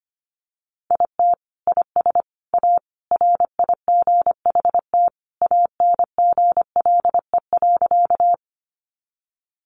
Morse code, heard as a keyed tone, about 25 wpm, 700 Hz. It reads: IT IS A RIGHT ANGLE.
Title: Sentences from Top 1500 Words 25wpm